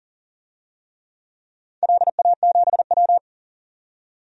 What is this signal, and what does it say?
Morse code, heard as a keyed tone, about 38 wpm, 700 Hz. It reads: LA7W
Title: DXCC-100 Call Signs - 4 Characters - 40wpm